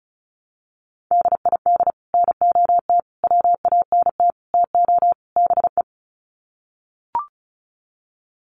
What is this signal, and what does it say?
Morse code, heard as a keyed tone, about 34 wpm, 700 Hz. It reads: DID NOT WANT TO BE
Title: Sets of 5 Words Encore 35wpm